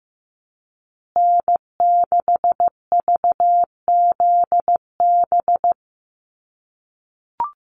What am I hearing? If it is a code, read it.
Morse code, heard as a keyed tone, about 15 wpm, 700 Hz. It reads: N6VZB
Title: USA Call Signs Repeated 3x 15wpm